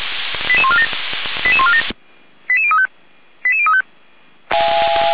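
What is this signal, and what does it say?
Five-tone selective call (ZVEI), heard as quick runs of five tones, tones of about 70 ms. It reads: 88147 88147 88035 88035
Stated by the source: Received on an SDR; ZVEI five-tone selective call 88141, fire brigade Orlamünde (feuerwehr-orlamuende).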